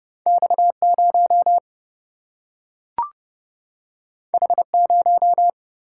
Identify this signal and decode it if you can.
Morse code, heard as a keyed tone, about 31 wpm, 700 Hz. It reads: X 0 H 0
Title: Single-Letter Single-Number 30wpm